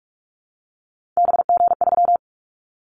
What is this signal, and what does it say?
Morse code, heard as a keyed tone, about 46 wpm, 700 Hz. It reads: BG3
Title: Callsign Prefixes for Top 100 DXCC - 45wpm